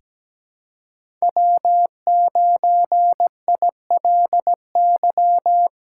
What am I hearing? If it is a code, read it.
Morse code, heard as a keyed tone, about 17 wpm, 700 Hz. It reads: W9ILY
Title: K1USN SST QSO Practice - 17wpm